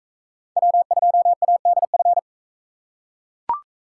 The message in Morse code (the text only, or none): W2ADF